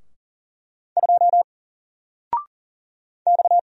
Morse code, heard as a keyed tone, about 40 wpm, 700 Hz. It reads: 2 X
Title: Single Letter-Number 40wpm